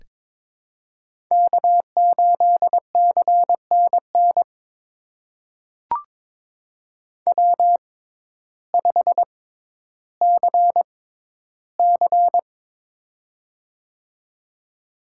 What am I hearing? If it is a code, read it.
Morse code, heard as a keyed tone, about 22 wpm, spaced as for 12 wpm, 700 Hz. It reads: K8CNN W5CC
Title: USA Call Signs - 6x Character Spacing - 22wpm